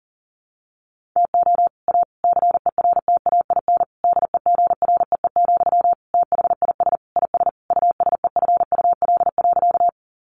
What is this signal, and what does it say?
Morse code, heard as a keyed tone, about 40 wpm, 700 Hz. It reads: TO A CERTAIN DEGREE, THIS IS USEFUL.
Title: Sentences from Top 1500 Words 40wpm